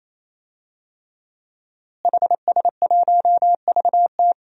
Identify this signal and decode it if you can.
Morse code, heard as a keyed tone, about 28 wpm, 700 Hz. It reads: HS1VT
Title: DXCC-100 Call Signs - 5 Characters - Repeated 3x - 28wpm